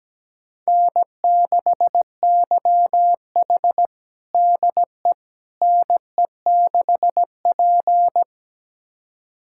Morse code, heard as a keyed tone, about 17 wpm, 700 Hz. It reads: N6YH DE NE6P
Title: USA Callsign DE Callsign - 4 Characters - 17wpm